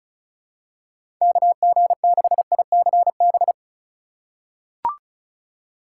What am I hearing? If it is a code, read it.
Morse code, heard as a keyed tone, about 34 wpm, 690 Hz. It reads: KG6ICB
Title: USA Call Signs - 6 Characters - 2x Character Spacing - 35wpm